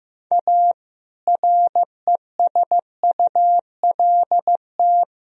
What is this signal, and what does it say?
Morse code, heard as a keyed tone, about 15 wpm, 700 Hz. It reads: A RESULT
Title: Speed-Racing - Sets of 3 Words 15wpm